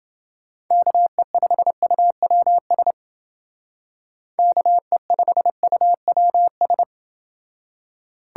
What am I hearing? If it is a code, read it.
Morse code, heard as a keyed tone, about 29 wpm, 700 Hz. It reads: KE5UWS KE5UWS